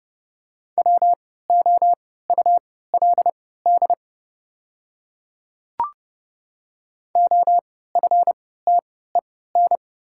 Morse code, heard as a keyed tone, about 30 wpm, spaced as for 17 wpm, 690 Hz. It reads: WOULD OFTEN
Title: 3x Character Spacing - Top 300 Words - 30wpm